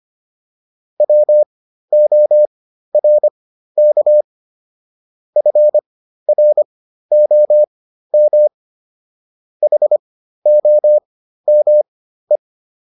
Morse code, heard as a keyed tone, about 25 wpm, spaced as for 13 wpm, 600 Hz. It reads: WORK FROM HOME